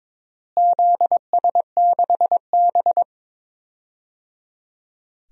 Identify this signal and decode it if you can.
Morse code, heard as a keyed tone, about 22 wpm, 700 Hz. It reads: ZS6B
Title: DXCC-100 Call Signs - 4 Characters - Repeated 3x - 22wpm